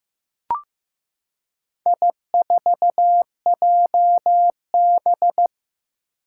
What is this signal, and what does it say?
Morse code, heard as a keyed tone, about 15 wpm, 700 Hz. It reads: I4JB
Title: Four Challenging Letter-Number Combinations 15wpm